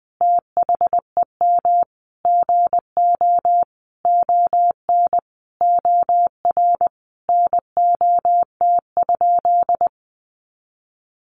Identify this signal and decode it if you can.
Morse code, heard as a keyed tone, about 20 wpm, 700 Hz. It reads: THEM GO ON OR NOT?
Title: Sentences from Top 600 Words 20wpm